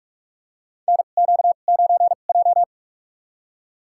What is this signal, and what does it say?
Morse code, heard as a keyed tone, about 41 wpm, spaced as for 31 wpm, 700 Hz. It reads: NQ9J
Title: USA Call Signs - 4 Characters - 2x Character Spacing - 45wpm